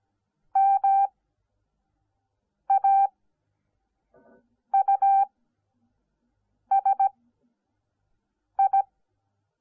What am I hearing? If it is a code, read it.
Morse code, heard as a keyed tone, about 17 wpm, spaced as for 5 wpm, 780 Hz. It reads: MAUSI